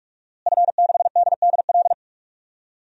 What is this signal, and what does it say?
Morse code, heard as a keyed tone, about 43 wpm, 700 Hz. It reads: F6DDL
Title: DXCC-100 Call Signs - 5 Characters - Repeated 3x - 45wpm